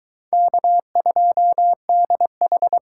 Morse code, heard as a keyed tone, about 22 wpm, 700 Hz. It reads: K2DH